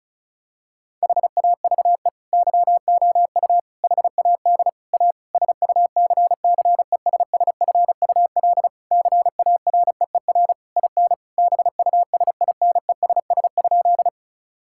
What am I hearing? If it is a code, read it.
Morse code, heard as a keyed tone, about 36 wpm, 700 Hz. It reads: HAVE YOU HAD A SUCCESSFUL CAREER IN BUSINESS?
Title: Sentences from Top 1000 Words - Review Part 6 - 35wpm